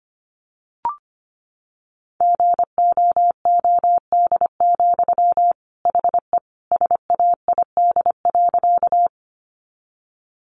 Morse code, heard as a keyed tone, about 25 wpm, 700 Hz. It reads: GOOD, HE SAID.
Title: Sentences from Top 200 Words 25wpm